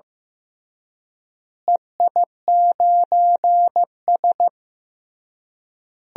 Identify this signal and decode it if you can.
Morse code, heard as a keyed tone, about 15 wpm, 700 Hz. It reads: EI9S
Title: DXCC-100 Call Signs - 4 Characters - Repeated 3x - 15wpm